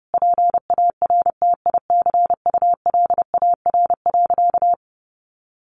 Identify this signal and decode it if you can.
Morse code, heard as a keyed tone, about 30 wpm, 700 Hz. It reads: PARTICULAR.
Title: Sentences from Top 600 Words 30wpm